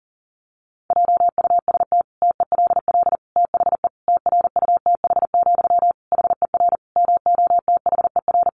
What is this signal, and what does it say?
Morse code, heard as a keyed tone, about 40 wpm, 700 Hz. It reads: JUST TELL THE TRUTH, HER MOTHER
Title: Sentences from Top 800 Words 40wpm